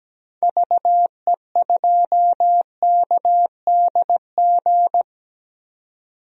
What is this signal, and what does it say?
Morse code, heard as a keyed tone, about 17 wpm, 700 Hz. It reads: VE2KDG
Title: DXCC-100 Call Signs - 6 Characters - 17wpm